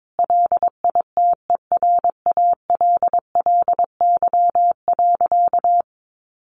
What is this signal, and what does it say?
Morse code, heard as a keyed tone, about 22 wpm, 700 Hz. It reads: LITERALLY.
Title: Sentences from Top 1400 Words 22wpm